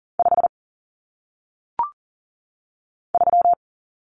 Morse code, heard as a keyed tone, about 40 wpm, 700 Hz. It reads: 5 3
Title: Numbers - 1-Digit 40wpm